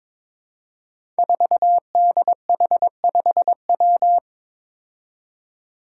Morse code, heard as a keyed tone, about 22 wpm, 700 Hz. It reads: 4DH5W
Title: Five Challenging Letter-Number Combinations 22wpm